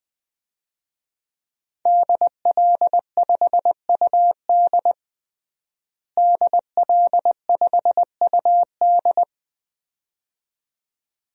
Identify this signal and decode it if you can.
Morse code, heard as a keyed tone, about 21 wpm, 700 Hz. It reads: DL5UD DL5UD
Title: DXCC-100 Call Signs - 5 Characters - Repeated 2x - 20wpm